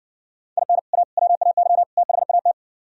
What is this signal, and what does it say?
Morse code, heard as a keyed tone, about 62 wpm, 700 Hz. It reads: IN A WAY THAT